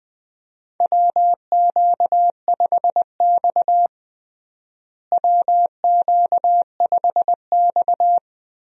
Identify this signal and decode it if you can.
Morse code, heard as a keyed tone, about 20 wpm, 700 Hz. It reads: WQ5X WQ5X
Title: USA Call Signs - 4 Characters - Repeated 2x - 20wpm